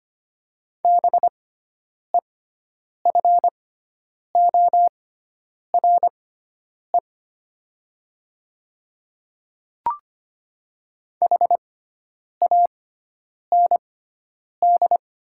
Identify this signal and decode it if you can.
Morse code, heard as a keyed tone, about 25 wpm, spaced as for 9 wpm, 700 Hz. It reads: BEFORE HAND
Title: 6x Character Spacing - Top 300 Words - 25wpm